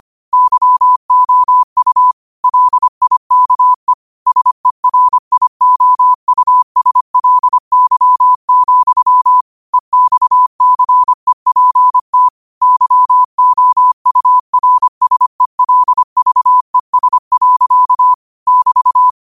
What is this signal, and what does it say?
Morse code, heard as a keyed tone, about 25 wpm, 1000 Hz. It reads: YOU LIKE SERIOUSLY, EXCEPT YOURSELVES. <BT>
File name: 25wpm